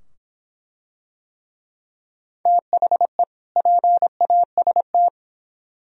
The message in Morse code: THE PAST

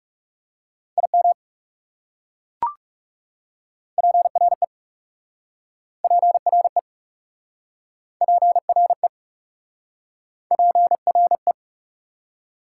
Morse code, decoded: IM PRE PRE PRE PRE

